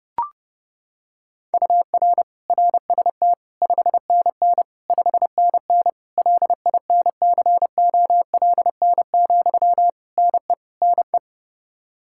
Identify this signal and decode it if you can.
Morse code, heard as a keyed tone, about 31 wpm, 700 Hz. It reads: UR RST 5NN 5NN LINCOLN, NE NE
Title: Practice QSO - Exchange 1 - Beginning - 30wpm